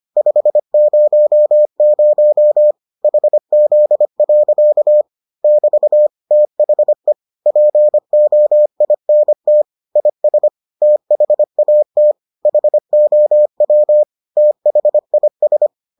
Morse code, heard as a keyed tone, about 25 wpm, 600 Hz. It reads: 500 HZ. <BT> THE POINT IS THAT HOW THIS